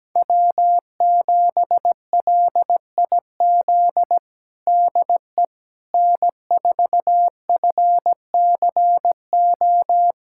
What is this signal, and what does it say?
Morse code, heard as a keyed tone, about 17 wpm, 700 Hz. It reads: W7LIZ DE N4FCO
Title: USA Callsign DE Callsign - 5 Characters - 17wpm